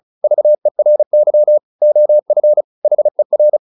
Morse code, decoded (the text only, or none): VERY OF HER